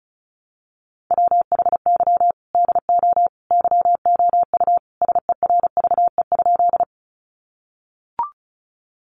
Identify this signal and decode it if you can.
Morse code, heard as a keyed tone, about 34 wpm, 700 Hz. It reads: WHY DO YOU SERVE?
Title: Sentences from Top 400 Words 35wpm